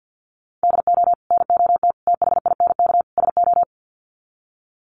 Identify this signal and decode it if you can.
Morse code, heard as a keyed tone, about 50 wpm, 700 Hz. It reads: DO NOT THINK SO
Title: Sets of 4 Words 50wpm